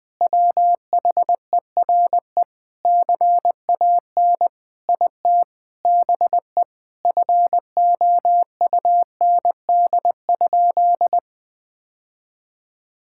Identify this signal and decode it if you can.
Morse code, heard as a keyed tone, about 20 wpm, 700 Hz. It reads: WHERE CAN IT BE FOUND?
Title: Sentences from Top 1600 Words 20wpm